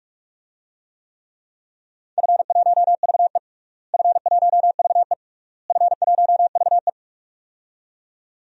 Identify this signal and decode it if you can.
Morse code, heard as a keyed tone, about 45 wpm, 700 Hz. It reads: F1VE F1VE F1VE
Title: DXCC-100 Call Signs - 4 Characters - Repeated 3x - 45wpm